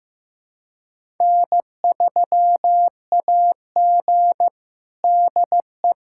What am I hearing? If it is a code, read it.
Morse code, heard as a keyed tone, about 15 wpm, 700 Hz. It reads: N3AG DE
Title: USA Callsign DE Callsign - 4 Characters - 15wpm